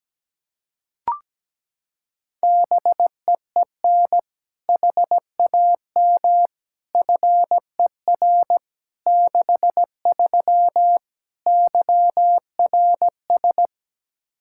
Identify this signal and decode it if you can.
Morse code, heard as a keyed tone, about 17 wpm, 700 Hz. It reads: BEEN HAM FER 63 YRS
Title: Practice QSO - Exchange 3 - More Info - 17wpm